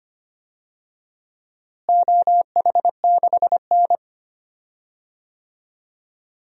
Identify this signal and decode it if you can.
Morse code, heard as a keyed tone, about 25 wpm, 700 Hz. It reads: OH6N
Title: DXCC-100 Call Signs - 4 Characters - Repeated 3x - 25wpm